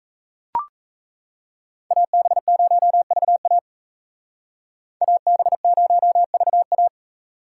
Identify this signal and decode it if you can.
Morse code, heard as a keyed tone, about 36 wpm, 700 Hz. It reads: AB0VA AB0VA